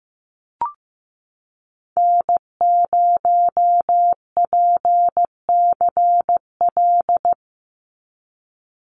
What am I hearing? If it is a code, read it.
Morse code, heard as a keyed tone, about 15 wpm, 700 Hz. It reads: N0PCL
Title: NA SOTA Activator Callsigns 15wpm